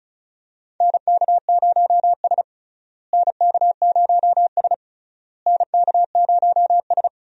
Morse code, heard as a keyed tone, about 34 wpm, 700 Hz. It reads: NK0S NK0S NK0S
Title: USA Call Signs - 4 Characters - Repeated 3x - 35wpm